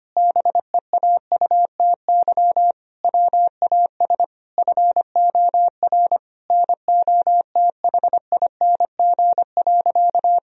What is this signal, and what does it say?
Morse code, heard as a keyed tone, about 25 wpm, 700 Hz. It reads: BEAUTY WAS FOR NOTHING.
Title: Sentences from Top 1000 Words - Review Part 2 - 25wpm